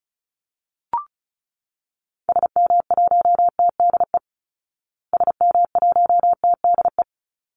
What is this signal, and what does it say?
Morse code, heard as a keyed tone, about 34 wpm, 700 Hz. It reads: SM1TDE SM1TDE
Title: Call Signs Repeated 3x - 35wpm